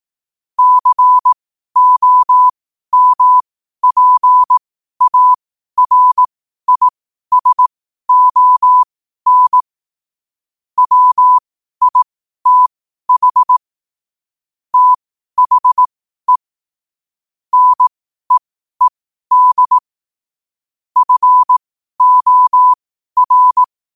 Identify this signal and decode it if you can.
Morse code, heard as a keyed tone, about 18 wpm, spaced as for 12 wpm, 1000 Hz. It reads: COMPARISON WITH THE NEED FOR
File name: qotdfiles / 12wpm